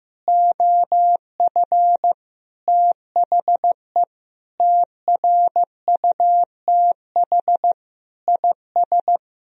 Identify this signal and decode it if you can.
Morse code, heard as a keyed tone, about 15 wpm, 700 Hz. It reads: OF THE TRUTH IS
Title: Sentences from Top 1000 Words - Review Part 3 - 15wpm